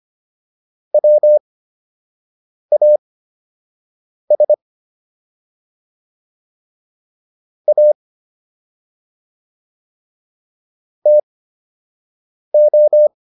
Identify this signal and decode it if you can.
Morse code, heard as a keyed tone, about 25 wpm, spaced as for 6 wpm, 600 Hz. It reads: WAS A TO